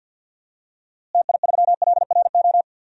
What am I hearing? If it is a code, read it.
Morse code, heard as a keyed tone, about 47 wpm, 700 Hz. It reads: TI3LRO
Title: DXCC-100 Call Signs - 6 Characters - Repeated 2x - 50wpm